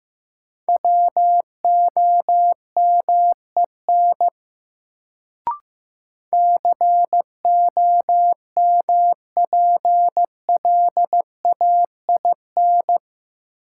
Morse code, heard as a keyed tone, about 15 wpm, 700 Hz. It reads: WOMEN COMPLAIN
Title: Sentences from Top 1300 Words - New Word Drill - 15wpm